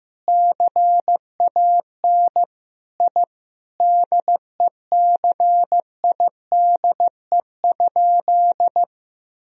Morse code, heard as a keyed tone, about 15 wpm, 700 Hz. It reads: CAN I DECIDE?
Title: Sentences from Top 1000 Words - Review Part 3 - 15wpm